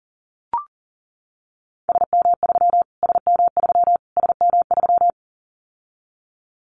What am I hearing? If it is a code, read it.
Morse code, heard as a keyed tone, about 40 wpm, 700 Hz. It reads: SM3 SM3 SM3
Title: Callsign Prefixes for Top 25 DXCC - Repeated 3x - 40wpm